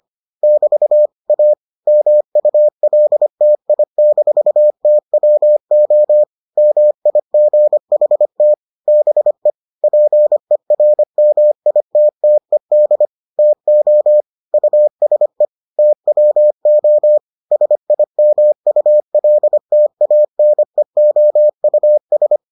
Morse code, heard as a keyed tone, about 25 wpm, 600 Hz. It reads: <BT> A MULTI-TWO MIGHT BE PERMITTED TO USE TWO SIMULTANEOUS